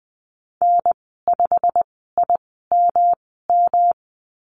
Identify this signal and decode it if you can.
Morse code, heard as a keyed tone, about 20 wpm, spaced as for 15 wpm, 700 Hz. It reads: N5IMM
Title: 128 Lesson 21g - Number 6 - Call Signs 20wpm